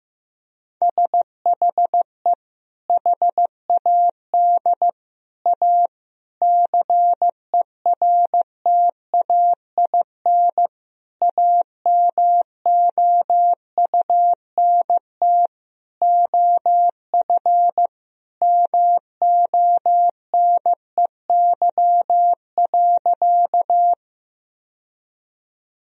Morse code, heard as a keyed tone, about 16 wpm, 700 Hz. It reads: SHE HAD A CERTAIN AMOUNT OF MONEY.
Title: Sentences from Top 1000 Words - Review Part 4 - 15wpm